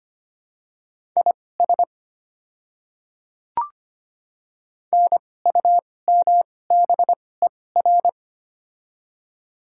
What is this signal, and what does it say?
Morse code, heard as a keyed tone, about 25 wpm, spaced as for 18 wpm, 700 Hz. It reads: IS NUMBER
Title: 2x Character Spacing - Top 100 Words - 25wpm